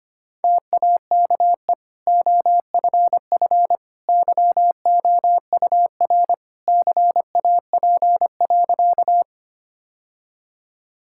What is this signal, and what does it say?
Morse code, heard as a keyed tone, about 25 wpm, 700 Hz. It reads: TAKE OFF YOUR CAP.
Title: Sentences from Top 1600 Words 25wpm